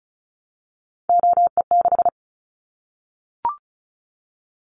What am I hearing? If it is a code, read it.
Morse code, heard as a keyed tone, about 34 wpm, 700 Hz. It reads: OE6
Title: Callsign Prefixes for Top 25 DXCC - Repeated 3x - 35wpm